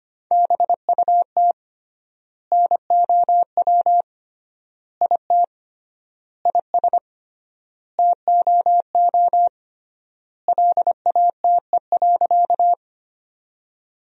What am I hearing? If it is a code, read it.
Morse code, heard as a keyed tone, about 25 wpm, 700 Hz. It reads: BUT NOW IT IS TOO LATE.
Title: 3x Word-Spacing - Sentences from Top 300 Words - 25wpm